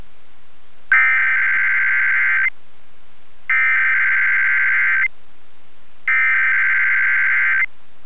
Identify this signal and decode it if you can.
Emergency Alert System (SAME) tones (header bursts), heard as harsh buzzing bursts. It is heard from 0.9 to 7.6 s.